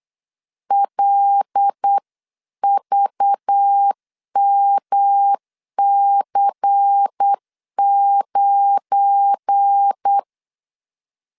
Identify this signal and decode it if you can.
Morse code, heard as a keyed tone, about 8 wpm, 800 Hz. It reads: LVMC9
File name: LEZIONI / 7 GF9Z / 7 GF9 8wpm 3545